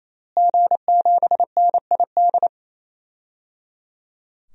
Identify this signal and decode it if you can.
Morse code, heard as a keyed tone, about 27 wpm, 700 Hz. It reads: G7NID
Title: DXCC-100 Call Signs - 5 Characters - Repeated 3x - 28wpm